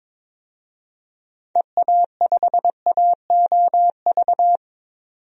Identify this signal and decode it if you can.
Morse code, heard as a keyed tone, about 21 wpm, 700 Hz. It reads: EA5AOV
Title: DXCC-100 Call Signs - 6 Characters - 22wpm